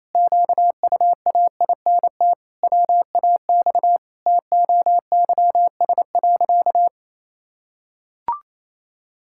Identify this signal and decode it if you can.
Morse code, heard as a keyed tone, about 28 wpm, 700 Hz. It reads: QUAINT WAX TOYS.
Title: Pangram Challenge v2 - 28wpm